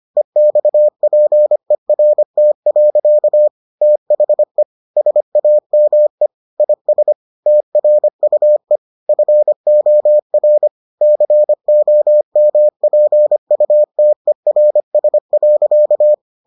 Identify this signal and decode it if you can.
Morse code, heard as a keyed tone, about 25 wpm, 600 Hz. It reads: EXPERT. THE SAME IS TRUE FOR COMPUTERS.